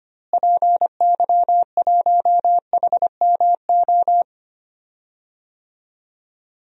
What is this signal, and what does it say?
Morse code, heard as a keyed tone, about 25 wpm, 700 Hz. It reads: PY1HMO
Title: DXCC-100 Call Signs - 6 Characters - Repeated 2x - 25wpm